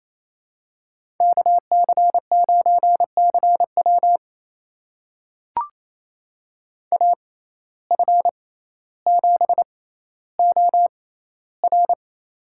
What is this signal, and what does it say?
Morse code, heard as a keyed tone, about 28 wpm, spaced as for 14 wpm, 700 Hz. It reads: KC9CW AF7OR
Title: USA Call Signs - 5 Characters - 6x Character Spacing - 28wpm